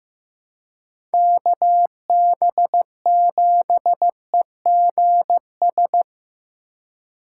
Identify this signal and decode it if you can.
Morse code, heard as a keyed tone, about 15 wpm, 700 Hz. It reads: KB7EGS
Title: USA Call Signs - 6 Characters - 2x Character Spacing - 15wpm